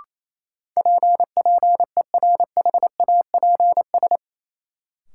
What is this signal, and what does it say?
Morse code, heard as a keyed tone, about 28 wpm, 700 Hz. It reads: PPERHAPS
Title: Sentences from Top 1100 Words - New Word Drill - 28wpm